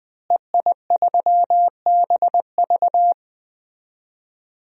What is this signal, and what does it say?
Morse code, heard as a keyed tone, about 21 wpm, 700 Hz. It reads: EI3BV
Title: DXCC-100 Call Signs - 5 Characters - Repeated 3x - 20wpm